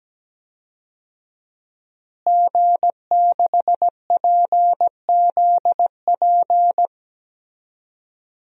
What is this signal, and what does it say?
Morse code, heard as a keyed tone, about 17 wpm, 700 Hz. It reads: G6PZP
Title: DXCC-100 Call Signs - 5 Characters - Repeated 2x - 17wpm